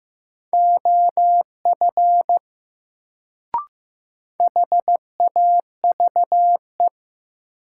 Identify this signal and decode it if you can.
Morse code, heard as a keyed tone, about 15 wpm, 700 Hz. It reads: OF HAVE
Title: Sentences from Top 100 Words - New Word Drill - 15wpm